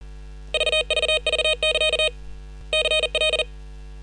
Morse code, heard as a keyed tone, about 40 wpm, 2500 Hz. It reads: VVV<KA> CL
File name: software / Morserino-32 / 40-WPM-VHSC